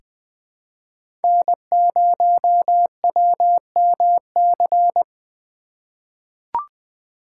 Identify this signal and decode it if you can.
Morse code, heard as a keyed tone, about 20 wpm, 700 Hz. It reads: N0WMC
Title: USA Call Signs - 4x Character Spacing - 20wpm